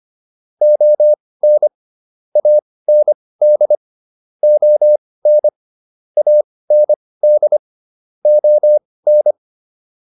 Morse code, heard as a keyed tone, about 25 wpm, spaced as for 18 wpm, 600 Hz. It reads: ON AND ON AND ON